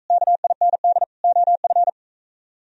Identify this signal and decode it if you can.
Morse code, heard as a keyed tone, about 44 wpm, 700 Hz. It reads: KIND OF